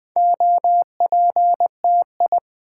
Morse code, heard as a keyed tone, about 20 wpm, 700 Hz. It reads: OPTI
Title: Build A Word - 2 Letters At A Time - 6-Letter Words - 20wpm